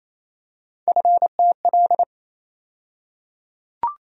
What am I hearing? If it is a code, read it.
Morse code, heard as a keyed tone, about 28 wpm, 700 Hz. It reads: FTL
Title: USA Call Sign Postfixes - 3 Letters 28wpm